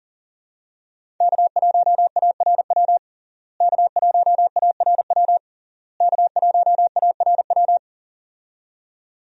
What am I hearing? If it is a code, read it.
Morse code, heard as a keyed tone, about 37 wpm, 700 Hz. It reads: K1ARW K1ARW K1ARW
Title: USA Call Signs - 5 Characters - Repeated 3x - 40wpm